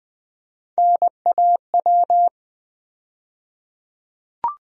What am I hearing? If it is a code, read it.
Morse code, heard as a keyed tone, about 20 wpm, 700 Hz. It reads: NAW
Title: USA Call Sign Postfixes - 3 Letters 20wpm